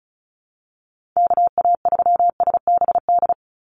Morse code, heard as a keyed tone, about 34 wpm, 700 Hz. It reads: KA3SBD
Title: USA Call Signs 35wpm